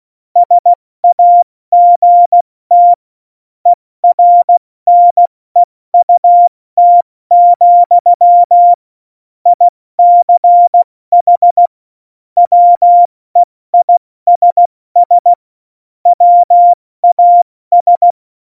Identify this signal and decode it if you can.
Morse code, heard as a keyed tone, about 16 wpm, 700 Hz. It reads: SAGT ERNEUT, ICH WEISS WAS
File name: Morsekurs / Witze / Witz 29. Januar 2019, 70 F80